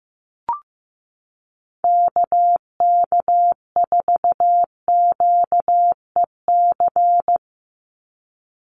Morse code, heard as a keyed tone, about 15 wpm, 700 Hz. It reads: KK4QEC
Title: USA Call Signs 15wpm